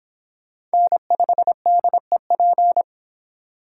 Morse code, heard as a keyed tone, about 25 wpm, 700 Hz. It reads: N5DEP